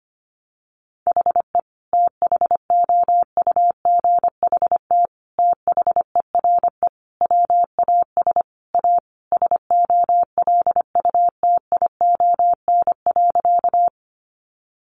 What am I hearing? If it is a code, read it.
Morse code, heard as a keyed tone, about 25 wpm, 700 Hz. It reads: HE THOUGHT THERE WAS A SOLUTION.
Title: Sentences from Top 1300 Words 25wpm